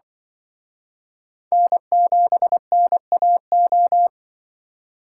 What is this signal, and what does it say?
Morse code, heard as a keyed tone, about 24 wpm, 700 Hz. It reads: N7NAO
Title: Speed Racing - USA 1x3 Call Signs 20wpm